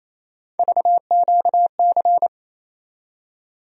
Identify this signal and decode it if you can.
Morse code, heard as a keyed tone, about 28 wpm, 700 Hz. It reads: VQC
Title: USA Call Sign Postfixes - 3 Letters 28wpm